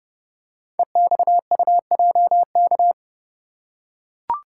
Morse code, heard as a keyed tone, about 30 wpm, 690 Hz. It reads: EXUJK
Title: Code Groups - 5 Letters - 30wpm